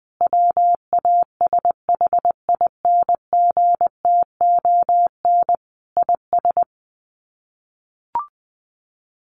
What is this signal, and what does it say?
Morse code, heard as a keyed tone, about 19 wpm, 700 Hz. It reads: WASHINGTON IS
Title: Sets of 5 Words Encore2 20wpm